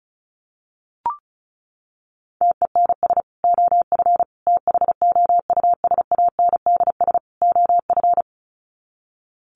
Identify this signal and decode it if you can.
Morse code, heard as a keyed tone, about 34 wpm, 700 Hz. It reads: TENS OF THOUSANDS OF
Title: Sets of 4 Words 35wpm